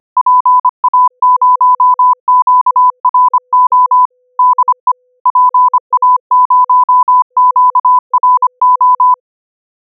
Morse code, heard as a keyed tone, about 25 wpm, 1000 Hz. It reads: PA0QRO DE PA0QRO
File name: SDR / proef4 / proef4